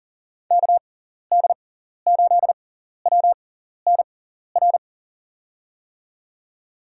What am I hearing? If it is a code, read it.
Morse code, heard as a keyed tone, about 40 wpm, spaced as for 14 wpm, 700 Hz. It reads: KD8WNR
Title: USA Call Signs - 6 Characters - 6x Character Spacing - 40wpm